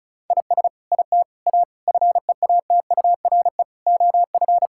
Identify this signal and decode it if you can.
Morse code, heard as a keyed tone, about 36 wpm, 700 Hz. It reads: IS IT A FEATURE OF
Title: Sentences from Top 1000 Words - Review Part 5 - 35wpm